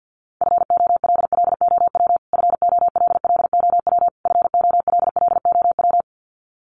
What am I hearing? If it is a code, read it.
Morse code, heard as a keyed tone, about 50 wpm, 700 Hz. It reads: FOLLOW FOLLOW FOLLOW
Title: Top 500 Words Repeated 50wpm